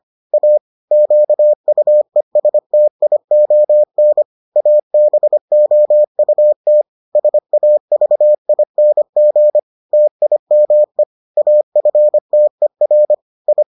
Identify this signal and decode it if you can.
Morse code, heard as a keyed tone, about 25 wpm, 600 Hz. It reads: A QUESTION ABOUT SAVING TIME AFTER I